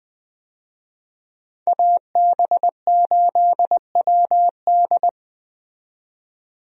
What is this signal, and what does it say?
Morse code, heard as a keyed tone, about 20 wpm, 700 Hz. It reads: AB8WD
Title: USA Call Signs - 5 Characters - 20wpm